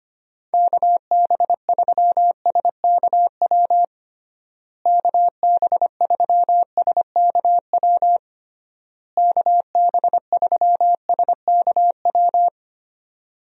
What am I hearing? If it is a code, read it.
Morse code, heard as a keyed tone, about 25 wpm, 700 Hz. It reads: KB3SKW KB3SKW KB3SKW
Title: USA Call Signs - 6 Characters - Repeated 3x - 25wpm